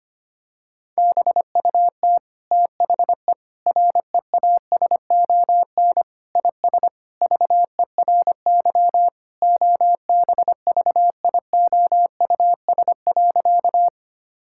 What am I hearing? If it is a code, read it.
Morse code, heard as a keyed tone, about 25 wpm, 700 Hz. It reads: BUT THE REASON IS VERY OBVIOUS.
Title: Sentences from Top 1600 Words 25wpm